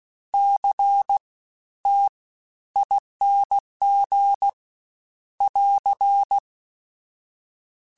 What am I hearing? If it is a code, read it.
Morse code, heard as a keyed tone, about 16 wpm, spaced as for 12 wpm, 800 Hz. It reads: CTING <AR>
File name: Special INGs-mp3s / Special INGs-mp3s / Fast ing @15